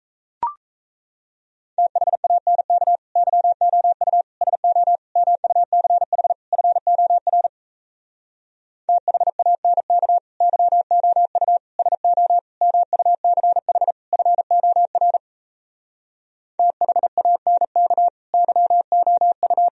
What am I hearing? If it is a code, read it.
Morse code, heard as a keyed tone, about 37 wpm, 700 Hz. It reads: THANK YOU SO MUCH FOR THANK YOU SO MUCH FOR THANK YOU